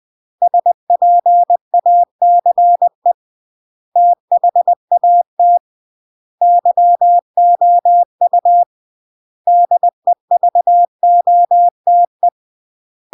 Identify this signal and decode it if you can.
Morse code, heard as a keyed tone, about 20 wpm, 700 Hz. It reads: SPACE THAT YOU DEVOTE
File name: CW / jodorowsky / 82 commandments 20-20